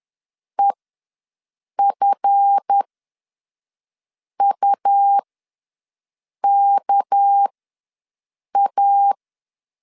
Morse code, heard as a keyed tone, about 10 wpm, spaced as for 5 wpm, 800 Hz. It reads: EFUKA
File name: LEZIONI / 7 GF9Z / 7 GF ws 10wpm 3918